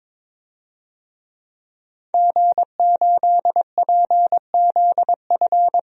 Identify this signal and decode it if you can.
Morse code, heard as a keyed tone, about 22 wpm, 700 Hz. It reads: G8PZF
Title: DXCC-100 Call Signs - 5 Characters - 22wpm